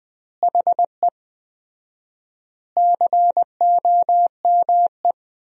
Morse code, heard as a keyed tone, about 21 wpm, 700 Hz. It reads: HE COME
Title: 4x Word-Spacing - Sentences from Top 300 Words - 20wpm